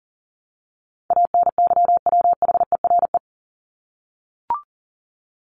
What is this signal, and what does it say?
Morse code, heard as a keyed tone, about 40 wpm, 700 Hz. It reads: ANYWHERE
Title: Top 1251-1500 Words 40wpm